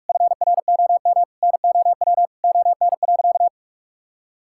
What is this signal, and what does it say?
Morse code, heard as a keyed tone, about 41 wpm, 700 Hz. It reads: FROM NOW ON.
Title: Sentences from Top 1000 Words - Review Part 6 - 45wpm